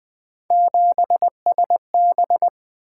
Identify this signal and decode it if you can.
Morse code, heard as a keyed tone, about 20 wpm, 700 Hz. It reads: 7SB
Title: Three Letter-Number Combinations - 20wpm